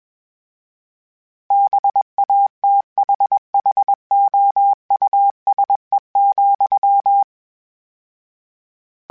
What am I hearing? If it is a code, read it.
Morse code, heard as a keyed tone, about 21 wpm, 800 Hz. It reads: BATHHOUSE,
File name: General / General-mp3s / Houses @20